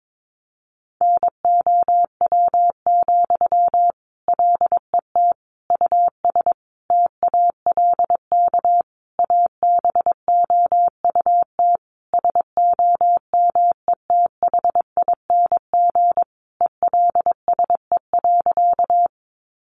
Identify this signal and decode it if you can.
Morse code, heard as a keyed tone, about 22 wpm, 700 Hz. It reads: NOW, LET US TALK ABOUT SOMETHING ELSE.
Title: Sentences from Top 800 Words 22wpm